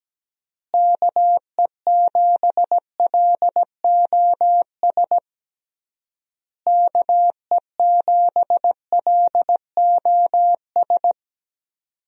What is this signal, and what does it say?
Morse code, heard as a keyed tone, about 17 wpm, 700 Hz. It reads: KE7LOS KE7LOS
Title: USA Call Signs - 6 Characters - Repeated 3x - 17wpm